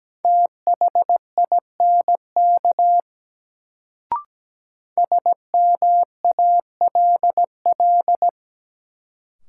Morse code, heard as a keyed tone, about 17 wpm, 700 Hz. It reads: THINK SMALL
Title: Sentences from Top 200 Words - New Word Drill - 17wpm